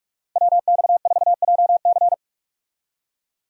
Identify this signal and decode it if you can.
Morse code, heard as a keyed tone, about 41 wpm, 700 Hz. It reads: WX4JC
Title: USA Call Signs - 5 Characters - 45wpm